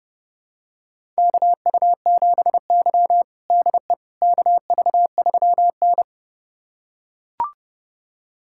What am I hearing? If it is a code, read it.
Morse code, heard as a keyed tone, about 30 wpm, 700 Hz. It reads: KU7Y DE KV3N
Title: USA Callsign DE Callsign - 4 Characters - 30wpm